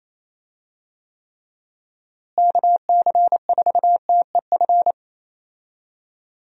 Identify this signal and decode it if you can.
Morse code, heard as a keyed tone, about 28 wpm, 700 Hz. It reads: KC4TEF
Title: USA Call Signs - 6 Characters - Repeated 2x - 28wpm